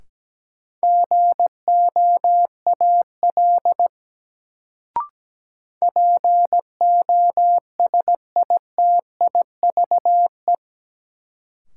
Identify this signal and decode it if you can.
Morse code, heard as a keyed tone, about 17 wpm, 700 Hz. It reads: GOAL POSITIVE
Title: Sentences from Top 1000 Words - Review Part 5 - New Word Drill - 17wpm